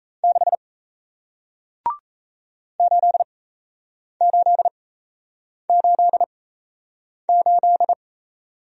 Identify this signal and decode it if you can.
Morse code, heard as a keyed tone, about 38 wpm, spaced as for 8 wpm, 700 Hz. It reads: 6 8888